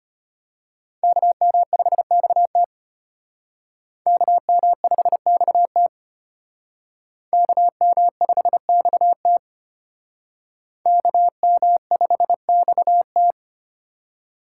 Morse code, heard as a keyed tone, about 31 wpm, 700 Hz. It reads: KM5XT KM5XT KM5XT KM5XT